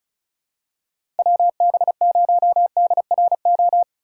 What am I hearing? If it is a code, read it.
Morse code, heard as a keyed tone, about 36 wpm, 700 Hz. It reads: WB0DRO
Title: USA Call Signs - 6 Characters - 2x Character Spacing - 35wpm